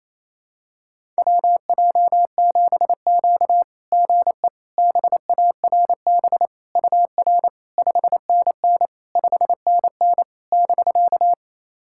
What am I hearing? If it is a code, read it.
Morse code, heard as a keyed tone, about 29 wpm, 660 Hz. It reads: WJ7Q GE BARB UR 5NN 5NN <BK>
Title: SOTA QSO Practice 28wpm